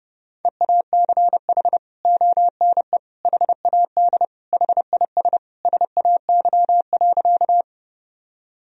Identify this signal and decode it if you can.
Morse code, heard as a keyed tone, about 32 wpm, 700 Hz. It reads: EACH ONE HAD HIS SAY.
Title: Sentences from Top 1000 Words - Review Part 1 - 30wpm